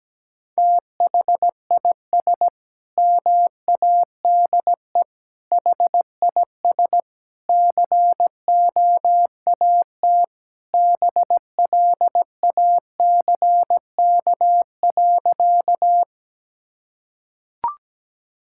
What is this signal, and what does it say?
Morse code, heard as a keyed tone, about 17 wpm, 700 Hz. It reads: THIS MADE HIS COAT BLACK.
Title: Sentences from Top 1600 Words 17wpm